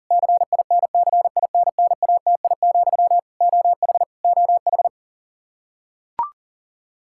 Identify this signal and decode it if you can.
Morse code, heard as a keyed tone, about 40 wpm, 700 Hz. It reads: CINCINNATI, OH OH
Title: Practice QSO - Exchange 1 - Beginning - 40wpm